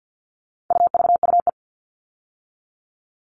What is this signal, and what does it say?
Morse code, heard as a keyed tone, about 50 wpm, 700 Hz. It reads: UVUE
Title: All Cut Numbers - 4-Digit 50wpm